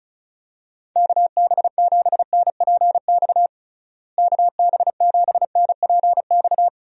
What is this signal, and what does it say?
Morse code, heard as a keyed tone, about 34 wpm, 700 Hz. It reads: KB7NPX KB7NPX
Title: USA Call Signs - 6 Characters - Repeated 3x - 35wpm